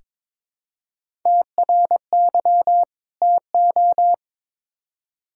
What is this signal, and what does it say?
Morse code, heard as a keyed tone, about 21 wpm, 700 Hz. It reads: TRY TO
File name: Build A Phrase - 2-Word Phrases - 22wpm